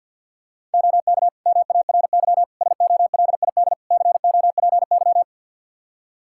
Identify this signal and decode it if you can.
Morse code, heard as a keyed tone, about 46 wpm, 700 Hz. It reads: OK MARY SOLID COPY
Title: Practice QSO - Exchange 3 - More Info - 50wpm